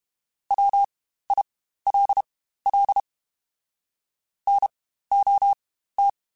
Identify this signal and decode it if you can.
Morse code, heard as a keyed tone, about 32 wpm, spaced as for 15 wpm, 800 Hz. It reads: WILL NOT
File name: Quotes Spaced / Quotes Spaced-mp3s / IgnoringFacts sp @30